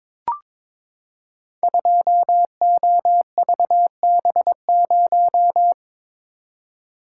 Morse code, heard as a keyed tone, about 21 wpm, 700 Hz. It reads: 2OVB0
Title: Code Groups - 5 Letter:Numbers - 22wpm